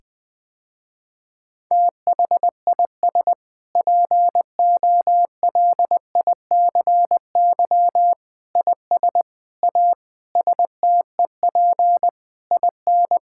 Morse code, heard as a keyed tone, about 20 wpm, 700 Hz. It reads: THIS POLICY IS A STEP IN
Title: Sentences with 6-Word Phrases - Spoken Beginning - 20wpm